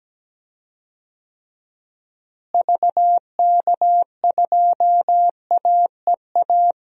Morse code, heard as a keyed tone, about 17 wpm, 700 Hz. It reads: VK2AEA
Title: DXCC-100 Call Signs - 6 Characters - Repeated 2x - 17wpm